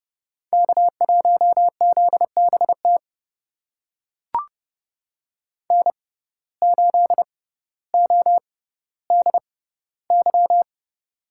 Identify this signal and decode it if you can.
Morse code, heard as a keyed tone, about 30 wpm, spaced as for 15 wpm, 700 Hz. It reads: K1ZBT N8ODY
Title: USA Call Signs - 5 Characters - 6x Character Spacing - 30wpm